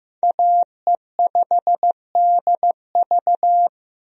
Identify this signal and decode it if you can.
Morse code, heard as a keyed tone, about 14 wpm, 700 Hz. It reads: AE5DV
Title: USA Call Signs - 5 Characters - Repeated 2x - 15wpm